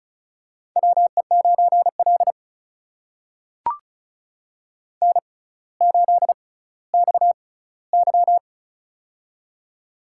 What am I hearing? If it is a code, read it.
Morse code, heard as a keyed tone, about 36 wpm, spaced as for 18 wpm, 700 Hz. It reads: WE9L N8XY
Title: USA Call Signs - 4 Characters - 6x Character Spacing - 35wpm